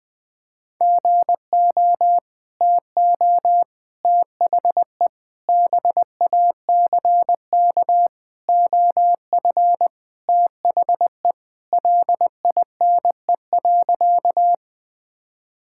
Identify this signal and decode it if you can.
Morse code, heard as a keyed tone, about 19 wpm, 700 Hz. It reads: GO TO THE BACK OF THE LINE.
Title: Sentences with 5-Word Phrases - Spoken Beginning - 20wpm